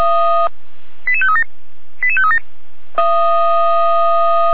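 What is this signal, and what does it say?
Five-tone selective call (ZVEI), heard as quick runs of five tones, tones of about 70 ms. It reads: 88537 88537
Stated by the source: ZVEI five-tone call received on SDR.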